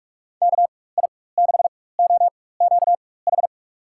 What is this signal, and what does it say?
Morse code, heard as a keyed tone, about 46 wpm, spaced as for 21 wpm, 700 Hz. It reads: KI6OQH
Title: USA Call Signs - 6 Characters - 4x Character Spacing - 45wpm